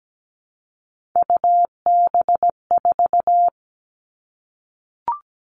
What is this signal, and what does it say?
Morse code, heard as a keyed tone, about 17 wpm, 700 Hz. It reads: UB4
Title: Three Challenging Letter-Number Combinations 17wpm